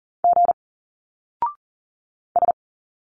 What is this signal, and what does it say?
Morse code, heard as a keyed tone, about 40 wpm, 700 Hz. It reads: G S
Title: Single Letters 40wpm